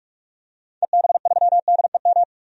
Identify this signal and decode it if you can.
Morse code, heard as a keyed tone, about 43 wpm, 700 Hz. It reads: EB3DEM
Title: DXCC-100 Call Signs - 6 Characters - 45wpm